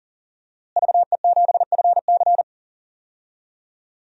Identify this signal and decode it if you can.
Morse code, heard as a keyed tone, about 38 wpm, 690 Hz. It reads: VE7FC
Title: DXCC-100 Call Signs - 5 Characters - Repeated 2x - 40wpm